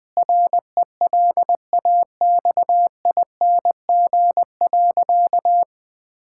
Morse code, heard as a keyed tone, about 20 wpm, 700 Hz. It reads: RELAXING.
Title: Difficult Sentences with Hints - 20wpm